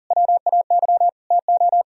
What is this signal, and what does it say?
Morse code, heard as a keyed tone, about 48 wpm, 700 Hz. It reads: WAY TO
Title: Speed-Racing - Sets of 4 Words 40wpm